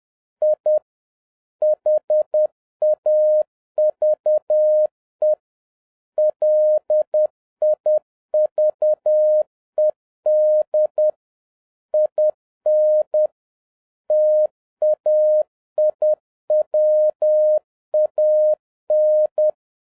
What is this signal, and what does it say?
Morse code, heard as a keyed tone, about 10 wpm, 610 Hz. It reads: I HAVE LIVED IN TAIWAN